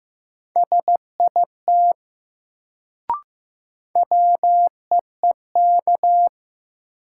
Morse code, heard as a keyed tone, about 15 wpm, 700 Hz. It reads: SIT WEEK
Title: Sentences from Top 500 Words - New Word Drill - 15wpm